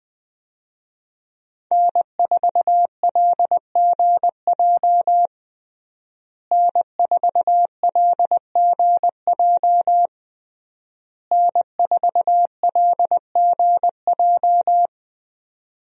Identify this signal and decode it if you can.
Morse code, heard as a keyed tone, about 20 wpm, 700 Hz. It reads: N4LGJ N4LGJ N4LGJ
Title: USA Call Signs - 5 Characters - Repeated 3x - 20wpm